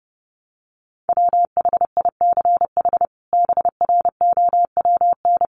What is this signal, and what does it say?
Morse code, heard as a keyed tone, about 30 wpm, 700 Hz. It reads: WHICH BROWN
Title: Sentences from Top 1200 Words 30wpm